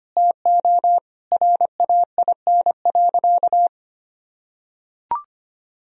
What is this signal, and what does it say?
Morse code, heard as a keyed tone, about 25 wpm, 700 Hz. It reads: TO RAIN.
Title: Sentences from Top 1000 Words - Review Part 1 - 25wpm